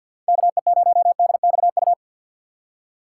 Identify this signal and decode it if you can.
Morse code, heard as a keyed tone, about 53 wpm, 700 Hz. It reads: KE0DXU
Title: USA Call Signs - 6 Characters - 4x Character Spacing - 50wpm